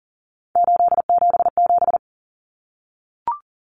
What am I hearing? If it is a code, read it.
Morse code, heard as a keyed tone, about 40 wpm, 700 Hz. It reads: 877
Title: Common Cut Numbers - 3-Digit 40wpm